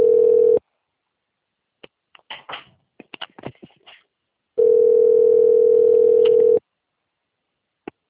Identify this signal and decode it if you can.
Ringback tone (North America), 440 + 480 Hz.